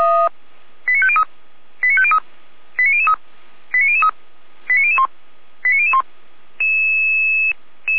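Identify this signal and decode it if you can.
Five-tone selective call (ZVEI), heard as quick runs of five tones, tones of about 70 ms. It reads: 89503 89503 89003 89003 89001 89001
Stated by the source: Five-tone ZVEI selective call 89060, received by SDR.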